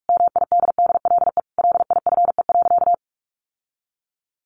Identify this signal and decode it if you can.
Morse code, heard as a keyed tone, about 44 wpm, 700 Hz. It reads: MIDDLE LIFE.
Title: Sentences from Top 1300 Words 45wpm